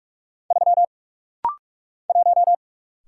Morse code, heard as a keyed tone, about 44 wpm, 690 Hz. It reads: 3 1
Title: Single Letter-Number - Rapid-Fire 45wpm